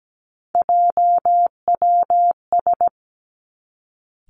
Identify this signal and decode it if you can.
Morse code, heard as a keyed tone, about 17 wpm, 700 Hz. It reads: JWS
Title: Three Challenging Letter-Number Combinations 17wpm